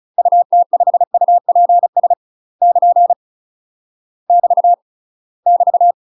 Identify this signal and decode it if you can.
Morse code, heard as a keyed tone, about 35 wpm, 700 Hz. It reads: UT5UPS <KN> <BT> <BT>
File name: UK MASTER QSOs TRAINING / 35WPM / UK MASTER QSOs 0038